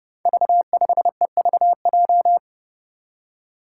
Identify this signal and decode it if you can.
Morse code, heard as a keyed tone, about 31 wpm, 700 Hz. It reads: V5EVJ
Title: Five Challenging Letter-Number Combinations 30wpm